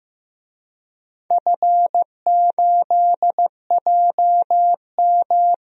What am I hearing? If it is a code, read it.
Morse code, heard as a keyed tone, about 15 wpm, 700 Hz. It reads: F8JM
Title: DXCC-100 Call Signs - 4 Characters - Repeated 2x - 15wpm